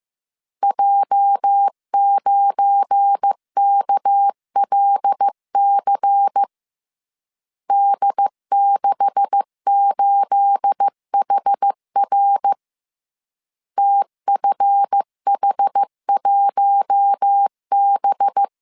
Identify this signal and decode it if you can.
Morse code, heard as a keyed tone, about 15 wpm, 800 Hz. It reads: J9KLC D68HR TFH1B
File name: CW / DET / 1 9 1500C S C 14wpm 3245